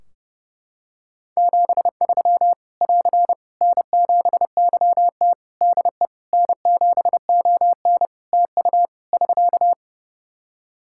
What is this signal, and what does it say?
Morse code, heard as a keyed tone, about 30 wpm, 700 Hz. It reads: 73 <AR> N7YT DE N7ON TU <SK>
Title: Practice QSO - Exchange 4 - Ending - 30wpm